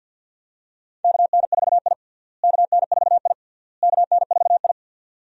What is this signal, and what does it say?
Morse code, heard as a keyed tone, about 49 wpm, 700 Hz. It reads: KN4I KN4I KN4I
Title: USA Call Signs - 4 Characters - Repeated 3x - 50wpm